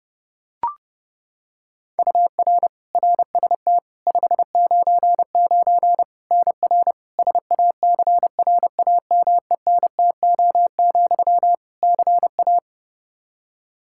Morse code, heard as a keyed tone, about 30 wpm, 700 Hz. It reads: UR RST 599 NR SACRAMENTO, CA
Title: Practice QSO - Exchange 1 - Beginning - 30wpm